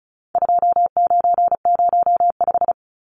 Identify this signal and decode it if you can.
Morse code, heard as a keyed tone, about 34 wpm, 700 Hz. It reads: 2905
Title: Numbers - 4-Digit 35wpm